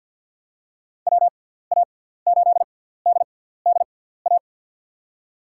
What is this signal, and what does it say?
Morse code, heard as a keyed tone, about 45 wpm, spaced as for 17 wpm, 700 Hz. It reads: WA8DDA
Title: USA Call Signs - 6 Characters - 6x Character Spacing - 50wpm